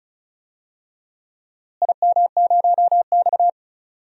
Morse code, heard as a keyed tone, about 34 wpm, 700 Hz. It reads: IM0X
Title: DXCC-100 Call Signs - 4 Characters - Repeated 3x - 35wpm